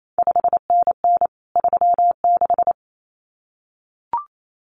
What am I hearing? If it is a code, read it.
Morse code, heard as a keyed tone, about 28 wpm, 700 Hz. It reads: 5NN 36
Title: CQ WW DX Contest Exchange Encore - 28wpm